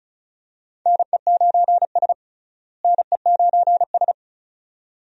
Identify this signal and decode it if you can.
Morse code, heard as a keyed tone, about 36 wpm, 700 Hz. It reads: NE9S NE9S
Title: USA Call Signs - 4 Characters - Repeated 2x - 35wpm